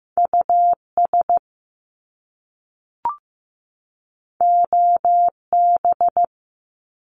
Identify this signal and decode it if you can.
Morse code, heard as a keyed tone, about 15 wpm, 700 Hz. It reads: US OB
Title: Top Two Letter Combinations in Words 15wpm